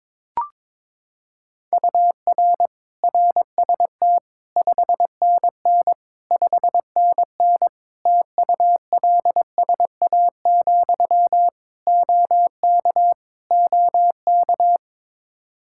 Morse code, heard as a keyed tone, about 22 wpm, 700 Hz. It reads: UR RST 5NN 5NN TULSA, OK OK
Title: Practice QSO - Exchange 1 - Beginning - 22wpm